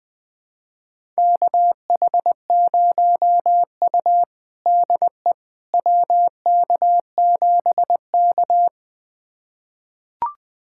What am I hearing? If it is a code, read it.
Morse code, heard as a keyed tone, about 20 wpm, 700 Hz. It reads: KH0U DE WK7K
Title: USA Callsign DE Callsign - 4 Characters - 20wpm